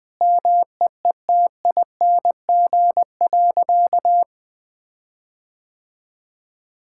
Morse code, heard as a keyed tone, about 20 wpm, 700 Hz. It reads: MEETING.
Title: Sentences with 6-Word Phrases - Spoken Beginning - 20wpm